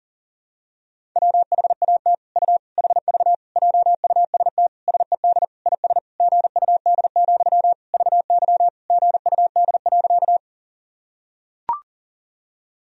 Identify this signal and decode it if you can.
Morse code, heard as a keyed tone, about 40 wpm, 700 Hz. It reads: WHAT U HV JUST SED IS GUD, VY GUD.
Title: Sentences from Top 200 Words and Abbreviations 40wpm